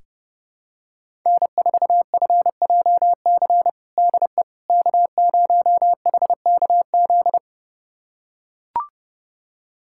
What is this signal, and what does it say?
Morse code, heard as a keyed tone, about 30 wpm, 700 Hz. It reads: N4FJC DE K0HKZ
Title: USA Callsign DE Callsign - 5 Characters - 30wpm